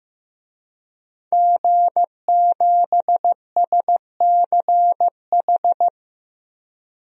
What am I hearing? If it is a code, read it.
Morse code, heard as a keyed tone, about 15 wpm, 700 Hz. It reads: G7SCH
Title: DXCC-100 Call Signs - 5 Characters - 15wpm